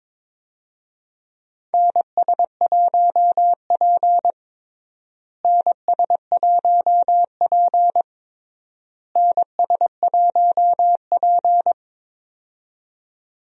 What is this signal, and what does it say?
Morse code, heard as a keyed tone, about 22 wpm, 700 Hz. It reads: NS1P NS1P NS1P
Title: USA Call Signs - 4 Characters - Repeated 3x - 22wpm